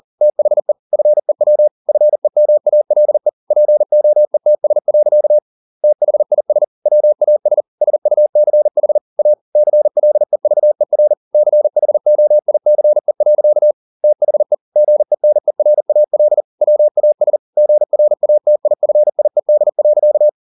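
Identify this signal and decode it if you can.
Morse code, heard as a keyed tone, about 40 wpm, 600 Hz. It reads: THE FEW FEMALE POETS. THIS WAS SUCH A CLEVER CHOICE. THE GENERAL WAS GRATIFIED.
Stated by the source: Morse code at 40 wpm, practice text, CW med Gnister.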